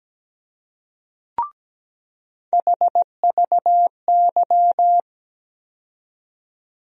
Morse code, heard as a keyed tone, about 17 wpm, 700 Hz. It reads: HVY
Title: USA Call Sign Postfixes - 3 Letters 17wpm